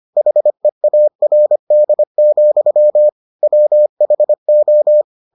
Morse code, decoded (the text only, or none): HEARD, WHO